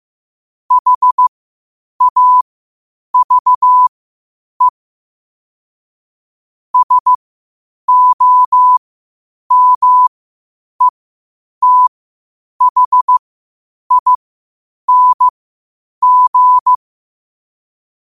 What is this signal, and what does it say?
Morse code, heard as a keyed tone, about 15 wpm, spaced as for 8 wpm, 1000 Hz. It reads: HAVE SOMETHING
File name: qotdfiles / 8wpm